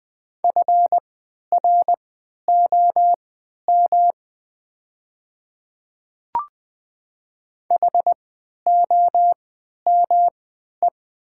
Morse code, heard as a keyed tone, about 20 wpm, spaced as for 11 wpm, 700 Hz. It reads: FROM HOME
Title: 3x Character Spacing - Top 300 Words - 20wpm